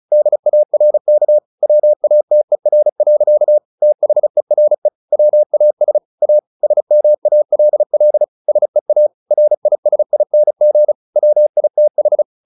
Morse code, heard as a keyed tone, about 34 wpm, 600 Hz. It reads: DARK WATER. THERE WAS A SMALL SEA RISING WITH
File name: the old man and the sea / the old man and the sea 1 35wpm